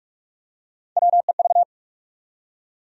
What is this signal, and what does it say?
Morse code, heard as a keyed tone, about 42 wpm, 700 Hz. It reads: WEV
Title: USA Call Sign Postfixes - 3 Letters 45wpm